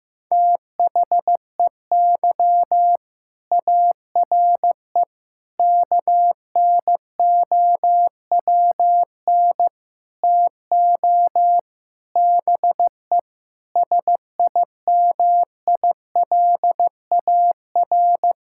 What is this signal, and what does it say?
Morse code, heard as a keyed tone, about 15 wpm, 700 Hz. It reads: THEY ARE KNOWN TO BE SIMILAR
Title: Sentences from Top 1000 Words - Review Part 5 - 15wpm